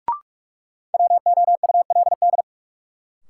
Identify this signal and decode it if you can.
Morse code, heard as a keyed tone, about 43 wpm, 700 Hz. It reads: WOULD
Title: Sentences from Top 100 Words - New Word Drill - 45wpm